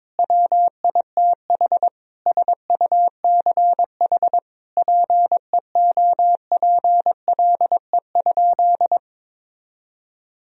Morse code, heard as a keyed tone, about 22 wpm, 700 Hz. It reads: WITH SUCH PEOPLE?
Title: Sentences from Top 1000 Words - Review Part 3 - 22wpm